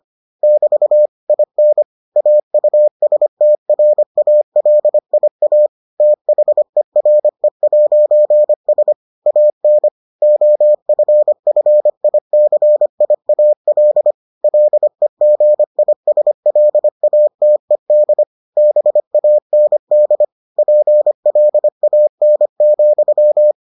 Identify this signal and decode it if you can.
Morse code, heard as a keyed tone, about 25 wpm, 600 Hz. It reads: <BT> IN AUSTRALIA THERE'S AN OFFICIAL LEGISLATED BAND PLAN,